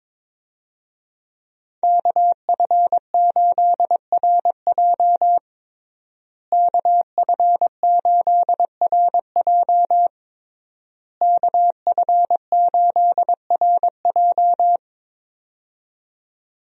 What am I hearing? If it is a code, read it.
Morse code, heard as a keyed tone, about 22 wpm, 700 Hz. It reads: KF8RJ KF8RJ KF8RJ
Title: USA Call Signs - 5 Characters - Repeated 3x - 22wpm